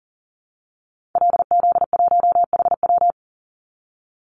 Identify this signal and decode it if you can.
Morse code, heard as a keyed tone, about 40 wpm, 700 Hz. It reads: LZ1HW
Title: Call Signs Repeated 3x - 40wpm